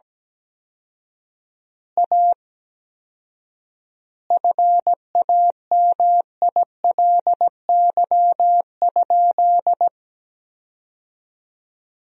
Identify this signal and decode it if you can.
Morse code, heard as a keyed tone, about 17 wpm, 700 Hz. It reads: A FAMILY?
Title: 4x Word-Spacing - Sentences from Top 400 Words - 17wpm